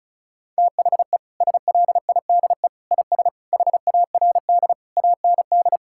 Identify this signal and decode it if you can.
Morse code, heard as a keyed tone, about 36 wpm, 700 Hz. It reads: THE SLIDE IS HARD AND
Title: Sentences from Top 1600 Words 35wpm